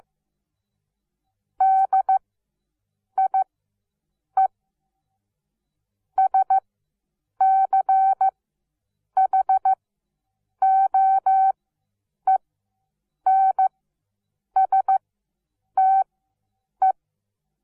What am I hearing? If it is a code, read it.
Morse code, heard as a keyed tone, about 15 wpm, spaced as for 8 wpm, 770 Hz. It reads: DIE SCHOENSTE
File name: Onlinelehrgang / morsen / DJ4UF Morsen17